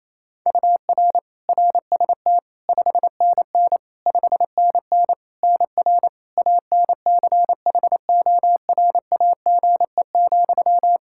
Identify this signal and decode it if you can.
Morse code, heard as a keyed tone, about 28 wpm, 700 Hz. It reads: UR RST 5NN 5NN NR ANCHORAGE,
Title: Practice QSO - Exchange 1 - Beginning - 28wpm